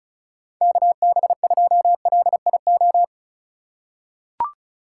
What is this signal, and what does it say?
Morse code, heard as a keyed tone, about 36 wpm, 700 Hz. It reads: KB2LIO
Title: USA Call Signs - 3x Character Spacing - 35wpm